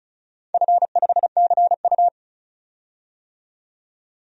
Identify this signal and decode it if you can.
Morse code, heard as a keyed tone, about 36 wpm, 700 Hz. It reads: F5CU
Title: DXCC-100 Call Signs - 4 Characters - 35wpm